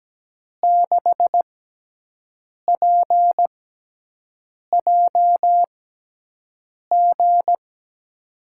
Morse code, heard as a keyed tone, about 17 wpm, spaced as for 6 wpm, 700 Hz. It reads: 6PJG
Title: USA Call Signs - 5 Characters - 6x Character Spacing - 17wpm